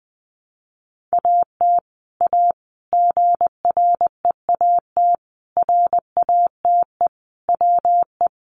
Morse code, heard as a keyed tone, about 20 wpm, 700 Hz. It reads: AT A GREAT RATE WE
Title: Sentences from Top 1200 Words 20wpm